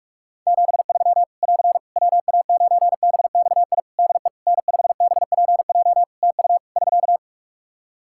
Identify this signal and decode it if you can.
Morse code, heard as a keyed tone, about 46 wpm, 700 Hz. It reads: 73 <AR> WA9BXI DE N5BPJ TU <SK>
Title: K1USN SST QSO Practice 45wpm